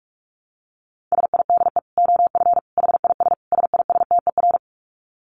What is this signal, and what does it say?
Morse code, heard as a keyed tone, about 44 wpm, 700 Hz. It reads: SIDE OF HIS SISTER
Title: Dit Challenge 45wpm